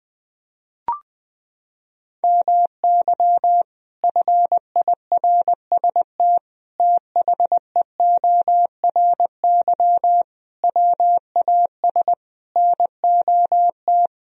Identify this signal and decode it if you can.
Morse code, heard as a keyed tone, about 20 wpm, 700 Hz. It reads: MY FIRST THEORY WAS NOT
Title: Sentences from Top 1000 Words - Review Part 3 - 20wpm